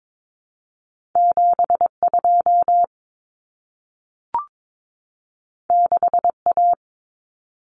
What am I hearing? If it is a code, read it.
Morse code, heard as a keyed tone, about 22 wpm, 700 Hz. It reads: 72 6A
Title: Common Cut Numbers - 2-Digit 22wpm